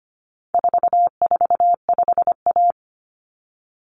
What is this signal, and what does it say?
Morse code, heard as a keyed tone, about 25 wpm, 700 Hz. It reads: 445A
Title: Common Cut Numbers - 4-Digit 25wpm